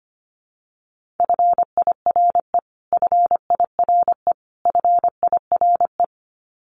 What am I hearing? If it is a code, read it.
Morse code, heard as a keyed tone, about 25 wpm, 700 Hz. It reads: FIRE FIRE FIRE
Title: Top 500 Words Repeated 25wpm